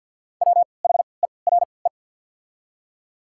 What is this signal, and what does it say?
Morse code, heard as a keyed tone, about 53 wpm, spaced as for 29 wpm, 700 Hz. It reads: WHERE
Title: 3x Character Spacing - Top 200 Words - 50wpm